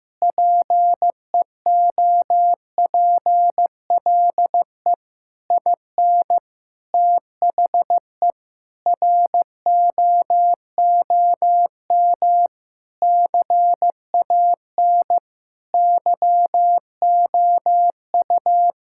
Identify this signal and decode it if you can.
Morse code, heard as a keyed tone, about 15 wpm, 700 Hz. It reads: PEOPLE IN THE ROOM CAN YOU
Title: Sentences from Top 1000 Words - Review Part 6 - 15wpm